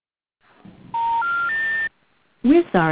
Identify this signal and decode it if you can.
Special information tone.